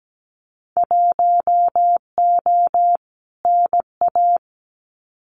Morse code, heard as a keyed tone, about 17 wpm, 700 Hz. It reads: 1O NA
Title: Winter Field Day Exchanges 17wpm